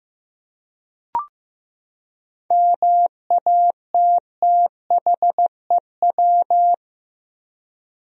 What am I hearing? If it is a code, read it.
Morse code, heard as a keyed tone, about 15 wpm, 700 Hz. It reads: MATTHEW
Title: USA Common Names - 15wpm